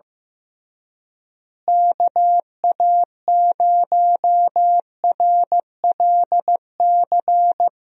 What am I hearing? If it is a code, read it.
Morse code, heard as a keyed tone, about 15 wpm, 700 Hz. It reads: KA0RLC
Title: USA Call Signs - 6 Characters - Repeated 3x - 15wpm